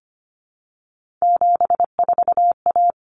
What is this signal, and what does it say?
Morse code, heard as a keyed tone, about 25 wpm, 700 Hz. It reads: 74A
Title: Common Cut Numbers - 3-Digit 25wpm